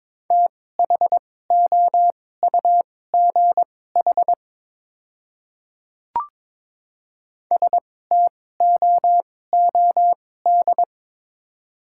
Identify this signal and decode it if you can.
Morse code, heard as a keyed tone, about 22 wpm, spaced as for 16 wpm, 700 Hz. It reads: THOUGH STOOD
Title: 2x Character Spacing - Top 500 Words - 22wpm